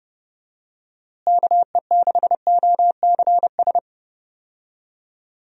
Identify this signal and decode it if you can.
Morse code, heard as a keyed tone, about 30 wpm, 700 Hz. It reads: KE6OCS
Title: USA Call Signs - 6 Characters - Repeated 3x - 30wpm